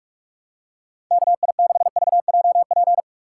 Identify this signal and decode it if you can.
Morse code, heard as a keyed tone, about 42 wpm, 700 Hz. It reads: KI6VJP
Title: USA Call Signs - 6 Characters - 4x Character Spacing - 45wpm